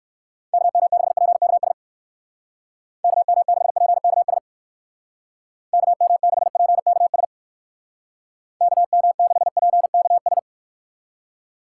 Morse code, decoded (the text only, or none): KM6PKS KM6PKS KM6PKS KM6PKS